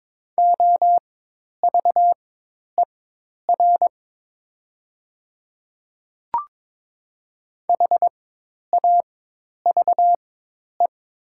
Morse code, heard as a keyed tone, about 22 wpm, spaced as for 10 wpm, 700 Hz. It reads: OVER HAVE
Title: 4x Character Spacing - Top 100 Words - 22wpm